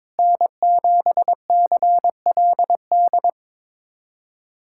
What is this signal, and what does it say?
Morse code, heard as a keyed tone, about 22 wpm, 700 Hz. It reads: N7CLD
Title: USA Callsign DE Callsign - 5 Characters - 22wpm